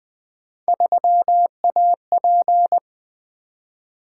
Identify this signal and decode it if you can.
Morse code, heard as a keyed tone, about 20 wpm, 700 Hz. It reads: 3AP
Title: Three Letter-Number Combinations - 20wpm